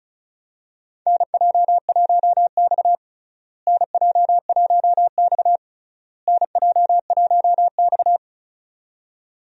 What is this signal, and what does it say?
Morse code, heard as a keyed tone, about 36 wpm, 700 Hz. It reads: NJ1X NJ1X NJ1X
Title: USA Call Signs - 4 Characters - Repeated 3x - 35wpm